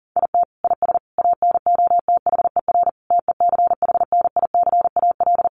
Morse code, heard as a keyed tone, about 40 wpm, 700 Hz. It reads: IT IS ANOTHER TECHNICAL
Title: Sentences from Top 1100 Words 40wpm